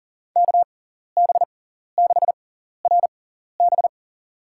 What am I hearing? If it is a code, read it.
Morse code, heard as a keyed tone, about 40 wpm, spaced as for 14 wpm, 700 Hz. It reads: KB6RB
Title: USA Call Signs - 5 Characters - 6x Character Spacing - 40wpm